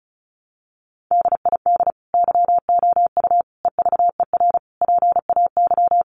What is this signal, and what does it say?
Morse code, heard as a keyed tone, about 34 wpm, 700 Hz. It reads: DID YOU EVER PAY
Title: Sentences from Top 700 Words 35wpm